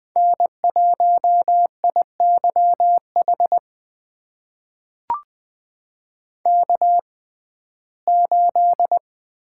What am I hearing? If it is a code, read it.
Morse code, heard as a keyed tone, about 21 wpm, spaced as for 15 wpm, 700 Hz. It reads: N1IYH K8
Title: USA Call Signs - 5 Characters - 6x Character Spacing - 20wpm